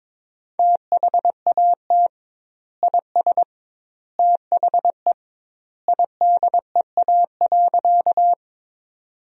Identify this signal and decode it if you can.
Morse code, heard as a keyed tone, about 22 wpm, 700 Hz. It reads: THAT IS THE IDEA.
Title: 2x Word-Spacing - Sentences from Top 300 Words - 22wpm